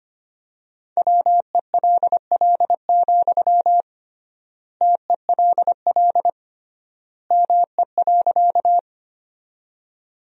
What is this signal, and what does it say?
Morse code, heard as a keyed tone, about 25 wpm, 700 Hz. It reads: WELL, TELL ME.
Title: 3x Word-Spacing - Sentences from Top 200 Words - 25wpm